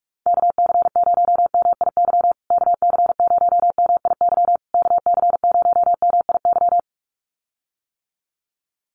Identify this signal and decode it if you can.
Morse code, heard as a keyed tone, about 44 wpm, 700 Hz. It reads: KC0MIY KC0MIY KC0MIY
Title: USA Call Signs Repeated 3x 45wpm